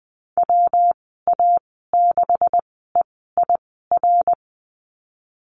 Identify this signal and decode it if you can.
Morse code, heard as a keyed tone, about 20 wpm, spaced as for 15 wpm, 700 Hz. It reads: WA6EIR
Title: 179 Lesson 29g - Slash - Call Signs 20wpm